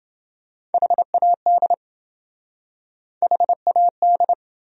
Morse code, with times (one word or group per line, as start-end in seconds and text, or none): 0.7-1.7 HAD
3.2-4.3 HAD